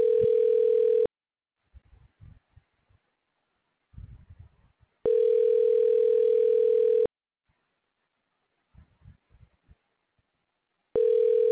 Ringback tone (North America), 440 + 480 Hz.